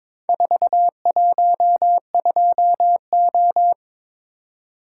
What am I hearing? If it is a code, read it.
Morse code, heard as a keyed tone, about 22 wpm, 700 Hz. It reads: 412O
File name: practice / 22wpm / Four Letter-Number Combinations - 22wpm